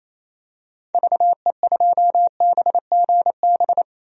Morse code, heard as a keyed tone, about 28 wpm, 700 Hz. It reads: VE2BGB
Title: DXCC-100 Call Signs - 6 Characters - 28wpm